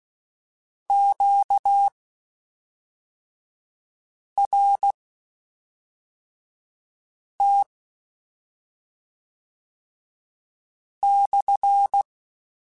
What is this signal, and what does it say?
Morse code, heard as a keyed tone, about 16 wpm, spaced as for 3 wpm, 800 Hz. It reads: QRT/
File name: Q Calls / Q Calls-mp3s / Q Calls SPSL1 @5